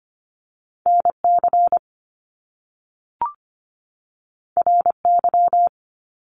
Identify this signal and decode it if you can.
Morse code, heard as a keyed tone, about 25 wpm, 700 Hz. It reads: NC RY
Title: Top Two Letter Combinations in Words 25wpm